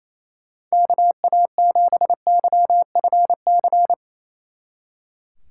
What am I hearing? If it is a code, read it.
Morse code, heard as a keyed tone, about 34 wpm, 700 Hz. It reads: KA7YFC
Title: Speed Racing - USA 2x3 Call Signs 28wpm